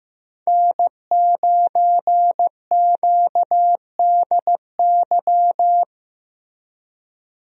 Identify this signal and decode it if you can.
Morse code, heard as a keyed tone, about 15 wpm, 700 Hz. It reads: N9QDY
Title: USA Call Signs - 5 Characters - Repeated 3x - 15wpm